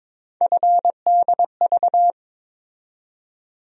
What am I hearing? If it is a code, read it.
Morse code, heard as a keyed tone, about 22 wpm, 700 Hz. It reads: FDV
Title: USA Call Sign Postfixes - 3 Letters 22wpm